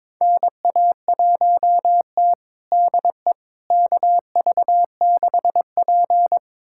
Morse code, heard as a keyed tone, about 22 wpm, 700 Hz. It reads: NA1T DE KV6P
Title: USA Callsign DE Callsign - 4 Characters - 22wpm